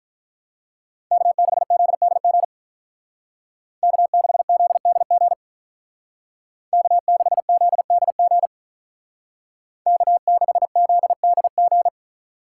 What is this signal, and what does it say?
Morse code, heard as a keyed tone, about 44 wpm, 700 Hz. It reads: K6ZDG K6ZDG K6ZDG K6ZDG